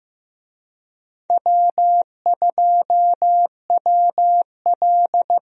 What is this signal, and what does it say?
Morse code, heard as a keyed tone, about 16 wpm, 700 Hz. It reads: W2WL
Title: USA Call Signs - 4 Characters - Repeated 3x - 15wpm